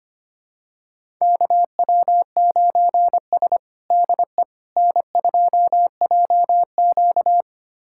Morse code, heard as a keyed tone, about 25 wpm, 700 Hz. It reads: KW9S DE N2JQ
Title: USA Callsign DE Callsign - 4 Characters - 25wpm